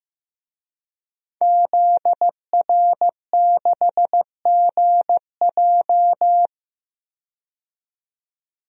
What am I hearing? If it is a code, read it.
Morse code, heard as a keyed tone, about 15 wpm, 700 Hz. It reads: ZR6GJ
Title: DXCC-100 Call Signs - 5 Characters - 15wpm